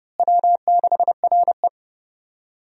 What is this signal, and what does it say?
Morse code, heard as a keyed tone, about 30 wpm, 700 Hz. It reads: W6RE
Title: USA Call Signs - 4 Characters - Repeated 3x - 30wpm